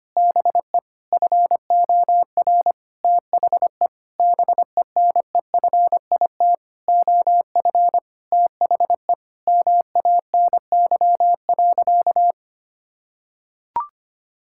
Morse code, heard as a keyed tone, about 25 wpm, 700 Hz. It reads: BE FOR THE BENEFIT OF THE MANY.
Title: Sentences from Top 1000 Words - Review Part 6 - 25wpm